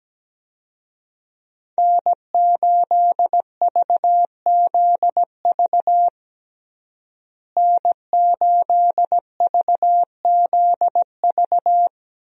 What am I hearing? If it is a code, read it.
Morse code, heard as a keyed tone, about 17 wpm, 700 Hz. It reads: N8VZV N8VZV
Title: USA Call Signs - 5 Characters - Repeated 3x - 17wpm